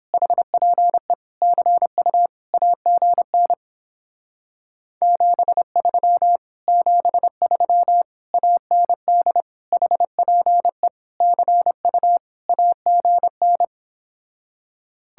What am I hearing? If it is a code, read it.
Morse code, heard as a keyed tone, about 29 wpm, 700 Hz. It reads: HPE CU AGN 73 73 AND HPE CU AGN